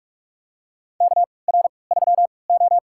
Morse code, heard as a keyed tone, about 44 wpm, spaced as for 25 wpm, 700 Hz. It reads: KR3O
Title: USA Call Signs - 4 Characters - 3x Character Spacing - 45wpm